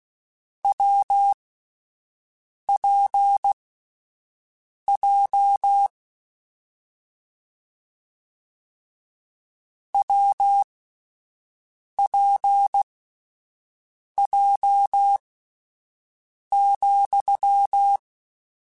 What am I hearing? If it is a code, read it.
Morse code, heard as a keyed tone, about 16 wpm, spaced as for 5 wpm, 800 Hz. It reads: WPJ WPJ,
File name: Difficult Combos / Difficult Combos-mp3s / WPJ WPJ1 @5